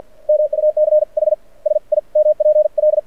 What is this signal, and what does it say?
Morse code, heard as a keyed tone, about 53 wpm, spaced as for 43 wpm, 610 Hz. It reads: MUCH SIMPL
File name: archived / CRSnet